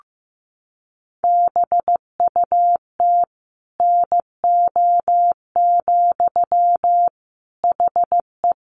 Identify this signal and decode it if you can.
Morse code, heard as a keyed tone, about 15 wpm, 700 Hz. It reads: BUT NO, HE
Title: Sentences from Top 1400 Words 15wpm